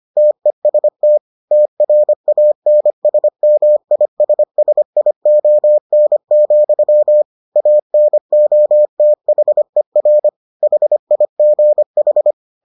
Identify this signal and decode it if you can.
Morse code, heard as a keyed tone, about 25 wpm, 600 Hz. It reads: TEST TRANSMISSION, ANOTHER HIGH